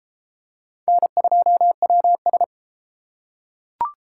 Morse code, heard as a keyed tone, about 31 wpm, 700 Hz. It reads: N2WS